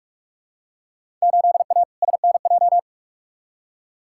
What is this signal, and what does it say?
Morse code, heard as a keyed tone, about 46 wpm, 700 Hz. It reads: 8A SNJ
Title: Field Day Exchanges Encore - 45wpm